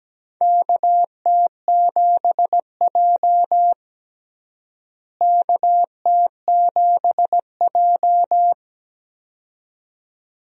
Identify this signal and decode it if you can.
Morse code, heard as a keyed tone, about 17 wpm, 700 Hz. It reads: KT7J KT7J
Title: USA Call Signs - 4 Characters - Repeated 2x - 17wpm